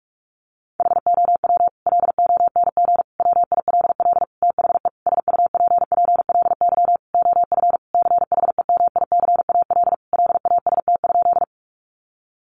Morse code, heard as a keyed tone, about 44 wpm, 700 Hz. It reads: HOW LONG WILL THE SUPPLY OF CHEMICAL LAST?
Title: Sentences from Top 1200 Words 45wpm